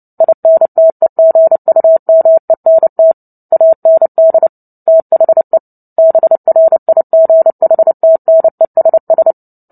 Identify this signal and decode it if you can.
Morse code, heard as a keyed tone, about 30 wpm, 650 Hz. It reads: INTEGUMENT AND THE BRIGHTNESS
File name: Book Two / 03-Chapter 3 - 29 WPM